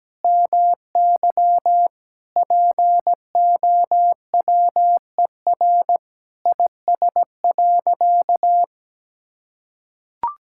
Morse code, heard as a keyed tone, about 17 wpm, 700 Hz. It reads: MY POWER IS.
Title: Sentences from Top 1000 Words - Review Part 6 - 17wpm